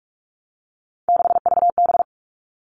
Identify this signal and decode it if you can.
Morse code, heard as a keyed tone, about 44 wpm, 700 Hz. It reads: 6VB
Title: All Cut Numbers - 3-Digit 45wpm